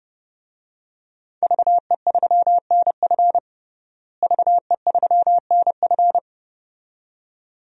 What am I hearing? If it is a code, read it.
Morse code, heard as a keyed tone, about 32 wpm, 700 Hz. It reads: VE3NF VE3NF
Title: DXCC-100 Call Signs - 5 Characters - Repeated 2x - 30wpm